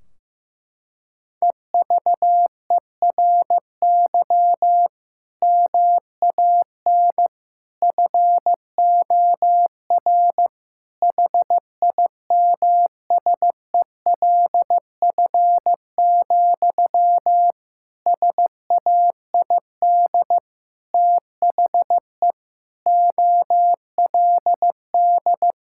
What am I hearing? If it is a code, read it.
Morse code, heard as a keyed tone, about 15 wpm, 700 Hz. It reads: EVERY MAN FOR HIMSELF, SAID THE OLD
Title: Sentences from Top 1600 Words 15wpm